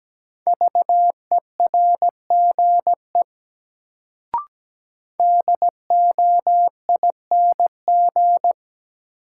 Morse code, heard as a keyed tone, about 17 wpm, 700 Hz. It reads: VERGE DOING
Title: Sets of 4 Words - New Word Drill - 17wpm